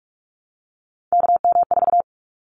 Morse code, heard as a keyed tone, about 46 wpm, 700 Hz. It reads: KM4
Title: Callsign Prefixes for Top 25 DXCC - 45wpm